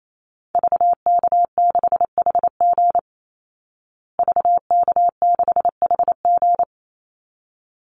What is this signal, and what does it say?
Morse code, heard as a keyed tone, about 28 wpm, 700 Hz. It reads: VK6HG VK6HG
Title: Call Signs Repeated 3x - 28wpm